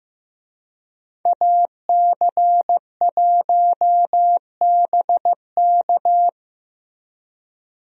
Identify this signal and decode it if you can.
Morse code, heard as a keyed tone, about 15 wpm, 700 Hz. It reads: AC1BK
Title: USA Call Signs - 5 Characters - Repeated 3x - 15wpm